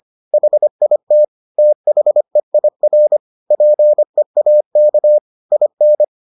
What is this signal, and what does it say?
Morse code, heard as a keyed tone, about 25 wpm, 600 Hz. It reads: HIT THEIR PEAK IN